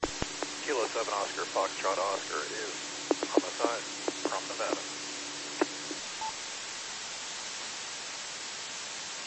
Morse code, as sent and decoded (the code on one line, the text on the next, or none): none